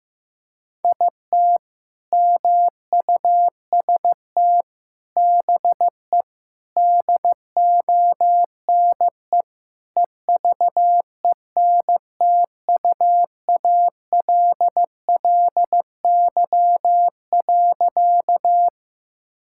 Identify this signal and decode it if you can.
Morse code, heard as a keyed tone, about 15 wpm, 700 Hz. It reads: IT MUST BE DONE EVENTUALLY.
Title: Sentences from Top 1000 Words - Review Part 6 - 15wpm